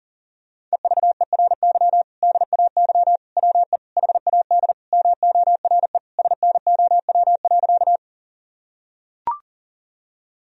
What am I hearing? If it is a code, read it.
Morse code, heard as a keyed tone, about 40 wpm, 700 Hz. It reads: EVERY DAY WE HAD MORE SNOW.
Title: Sentences from Top 1000 Words - Review Part 3 - 40wpm